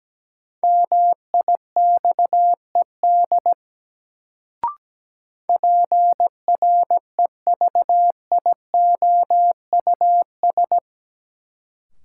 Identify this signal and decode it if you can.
Morse code, heard as a keyed tone, about 17 wpm, 700 Hz. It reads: MIXED PREVIOUS
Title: Sentences from Top 1500 Words - New Word Drill Alt - 17wpm